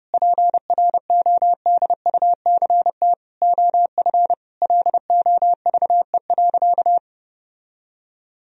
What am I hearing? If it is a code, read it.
Morse code, heard as a keyed tone, about 32 wpm, 700 Hz. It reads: PRODUCT OF LOVE.
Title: Sentences from Top 1000 Words - Review Part 2 - 30wpm